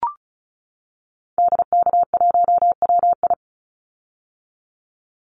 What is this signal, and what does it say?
Morse code, heard as a keyed tone, about 36 wpm, 700 Hz. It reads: DK1WI
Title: Call Signs - 35wpm